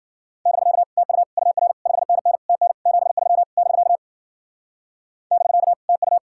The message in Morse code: <BK> TU UR 5NN TN 73 <BK> <BK> TU

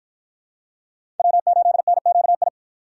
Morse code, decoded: W8NQI

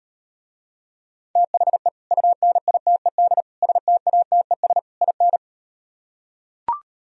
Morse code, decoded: THE UNITED STATES IN

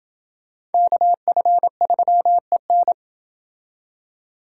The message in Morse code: KF3EN

